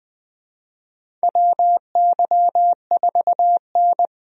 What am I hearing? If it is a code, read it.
Morse code, heard as a keyed tone, about 20 wpm, 700 Hz. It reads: WY4N
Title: USA Call Signs - 4 Characters - Repeated 2x - 20wpm